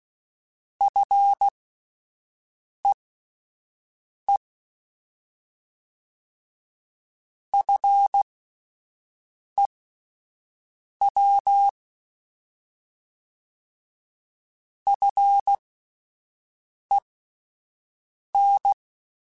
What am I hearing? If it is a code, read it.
Morse code, heard as a keyed tone, about 16 wpm, spaced as for 5 wpm, 800 Hz. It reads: FEE FEW FEN
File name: Alphabetical / Alphabetical mp3s / Fs @5